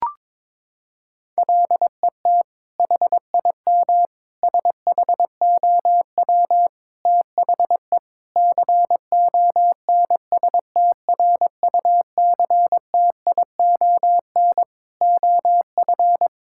Morse code, heard as a keyed tone, about 22 wpm, 700 Hz. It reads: LET HIM SHOW THE CONSTRUCTION OF
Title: Sentences from Top 1600 Words 22wpm